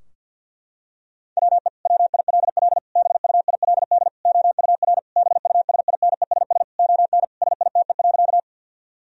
Morse code, heard as a keyed tone, about 52 wpm, 700 Hz. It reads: WE WILL BUILD OUR BUSINESS ON SITE.
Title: Sentences from Top 1000 Words - Review Part 4 - 50wpm